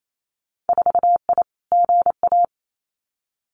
Morse code, heard as a keyed tone, about 28 wpm, 690 Hz. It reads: 4I GA
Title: Winter Field Day Exchanges Encore 28wpm